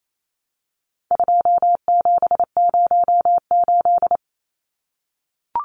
Morse code, heard as a keyed tone, about 28 wpm, 700 Hz. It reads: 2708
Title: Numbers - 4-Digit 28wpm